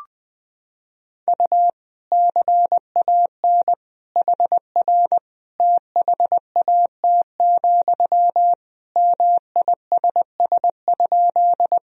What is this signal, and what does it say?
Morse code, heard as a keyed tone, about 21 wpm, 700 Hz. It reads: U CAN HR THAT, MISS?
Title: Sentences from Top 500 Words and Abbreviations 20wpm